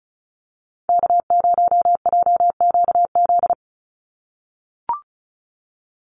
Morse code, heard as a keyed tone, about 34 wpm, 700 Hz. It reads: K0JQZ
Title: NA SOTA Activator Callsigns 35wpm